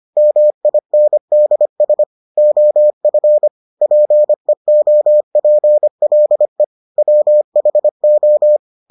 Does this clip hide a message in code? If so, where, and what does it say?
Yes: Morse code from 0.2 to 8.6 s, heard as a keyed tone, about 25 wpm, 600 Hz. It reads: MINDS OF PEOPLE WHO